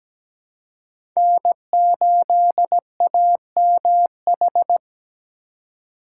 Morse code, heard as a keyed tone, about 17 wpm, 700 Hz. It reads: N8AMH
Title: USA Call Signs - 5 Characters - 3x Character Spacing - 17wpm